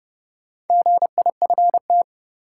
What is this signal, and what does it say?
Morse code, heard as a keyed tone, about 30 wpm, 700 Hz. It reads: GIFT
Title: Sentences from Top 900 Words - New Word Drill - 30wpm